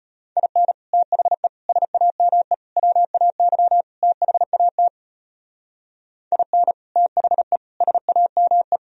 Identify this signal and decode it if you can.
Morse code, heard as a keyed tone, about 31 wpm, 700 Hz. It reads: IN THE SAME WAY THAT IN THE SAME
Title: Speed-Racing - Sets of 5 Words 25wpm